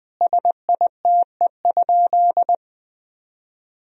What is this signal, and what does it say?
Morse code, heard as a keyed tone, about 20 wpm, 700 Hz. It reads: SITE?
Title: Sentences from Top 1000 Words - Review Part 4 - 20wpm